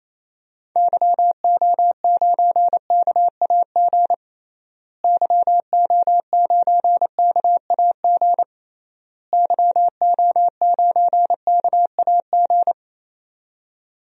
Morse code, heard as a keyed tone, about 28 wpm, 700 Hz. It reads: YO9KAG YO9KAG YO9KAG
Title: DXCC-100 Call Signs - 6 Characters - Repeated 3x - 28wpm